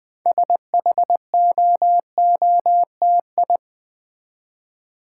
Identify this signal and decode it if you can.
Morse code, heard as a keyed tone, about 20 wpm, 700 Hz. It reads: SHOOTI
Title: Build A Word - 3 Letters At A Time - 8-Letter Words - 20wpm